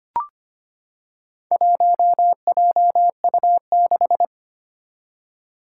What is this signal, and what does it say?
Morse code, heard as a keyed tone, about 25 wpm, 700 Hz. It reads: 1JU6
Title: Four Challenging Letter-Number Combinations 25wpm